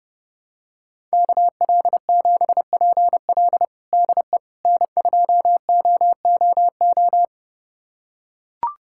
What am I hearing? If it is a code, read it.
Morse code, heard as a keyed tone, about 29 wpm, 700 Hz. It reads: KL7PL DE N2OOO
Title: USA Callsign DE Callsign - 5 Characters - 30wpm